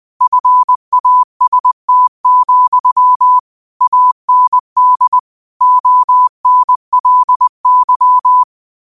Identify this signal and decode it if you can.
Morse code, heard as a keyed tone, about 20 wpm, 1000 Hz. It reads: FAST, AND ONLY
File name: qotdfiles / 20wpm